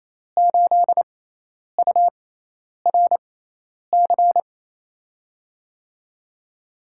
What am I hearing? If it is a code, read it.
Morse code, heard as a keyed tone, about 28 wpm, spaced as for 10 wpm, 700 Hz. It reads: 8URC
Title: USA Call Signs - 6 Characters - 6x Character Spacing - 28wpm